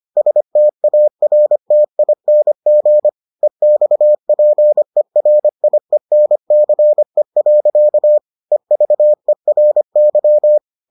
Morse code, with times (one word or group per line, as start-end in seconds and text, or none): 0.2-3.1 STARTING
3.4-8.2 EXPERIENCE.
8.5-10.6 EVERY